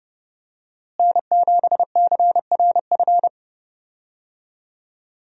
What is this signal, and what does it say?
Morse code, heard as a keyed tone, about 31 wpm, 700 Hz. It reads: N7CRF
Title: USA Call Signs - 5 Characters - Repeated 3x - 30wpm